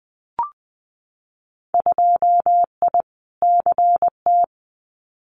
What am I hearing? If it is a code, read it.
Morse code, heard as a keyed tone, about 20 wpm, 700 Hz. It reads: 2I CT
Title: Winter Field Day Exchanges 20wpm